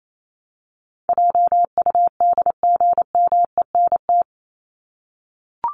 Morse code, heard as a keyed tone, about 28 wpm, 700 Hz. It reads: JUDGMENT
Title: Top 1751-2000 Words 28wpm